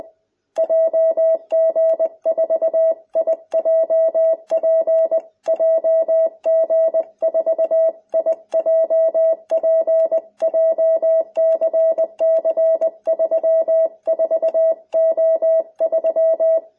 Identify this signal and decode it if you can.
Morse code, heard as a keyed tone, about 20 wpm, 650 Hz. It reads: JG4IJP JG4IJP JCC34O3